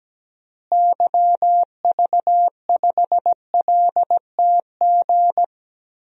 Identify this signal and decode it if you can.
Morse code, heard as a keyed tone, about 17 wpm, 700 Hz. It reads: YV5LTG
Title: DXCC-100 Call Signs - 6 Characters - Repeated 3x - 17wpm